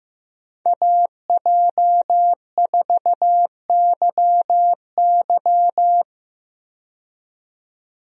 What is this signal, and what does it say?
Morse code, heard as a keyed tone, about 16 wpm, 700 Hz. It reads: AJ4YY
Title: USA Call Signs - 5 Characters - 15wpm